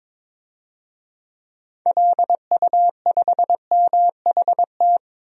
Morse code, heard as a keyed tone, about 22 wpm, 700 Hz. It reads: LU5MHT
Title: DXCC-100 Call Signs - 6 Characters - Repeated 2x - 22wpm